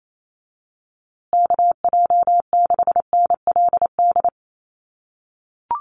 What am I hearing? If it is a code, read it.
Morse code, heard as a keyed tone, about 28 wpm, 700 Hz. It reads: KJ6NLD
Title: USA Call Signs Repeated 2x 28wpm